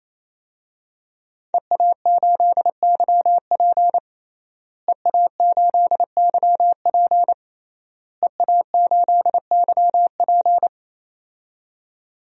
Morse code, heard as a keyed tone, about 28 wpm, 700 Hz. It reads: EA8YP EA8YP EA8YP
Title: DXCC-100 Call Signs - 5 Characters - Repeated 3x - 28wpm